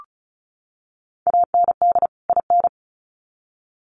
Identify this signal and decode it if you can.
Morse code, heard as a keyed tone, about 34 wpm, 700 Hz. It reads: AND IN
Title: Sets of 2 Words 35wpm